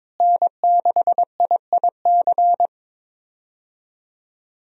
Morse code, heard as a keyed tone, about 21 wpm, 700 Hz. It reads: N6IIC
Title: USA Call Signs - 5 Characters - Repeated 3x - 22wpm